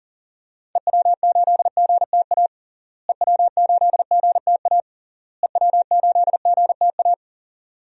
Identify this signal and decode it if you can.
Morse code, heard as a keyed tone, about 40 wpm, 700 Hz. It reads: EW8GTA EW8GTA EW8GTA
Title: DXCC-100 Call Signs - 6 Characters - Repeated 3x - 40wpm